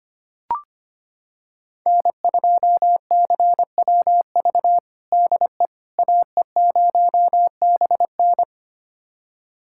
Morse code, heard as a keyed tone, about 25 wpm, 700 Hz. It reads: N2CWV DE AE0BN
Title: USA Callsign DE Callsign - 5 Characters - 25wpm